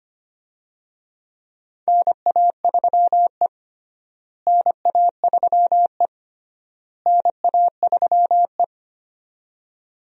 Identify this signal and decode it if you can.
Morse code, heard as a keyed tone, about 25 wpm, 700 Hz. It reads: NA3E NA3E NA3E
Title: USA Call Signs - 4 Characters - Repeated 3x - 25wpm